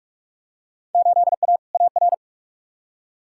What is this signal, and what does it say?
Morse code, heard as a keyed tone, about 46 wpm, 700 Hz. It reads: 8A AR
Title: Field Day Exchanges Encore - 45wpm